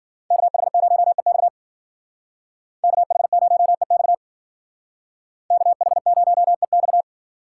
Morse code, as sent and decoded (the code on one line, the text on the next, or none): -.- .... ----- . -..- / -.- .... ----- . -..- / -.- .... ----- . -..-
KH0EX KH0EX KH0EX